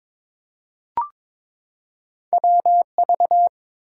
Morse code, heard as a keyed tone, about 23 wpm, 700 Hz. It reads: WV
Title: US States and Canadian Provinces 22wpm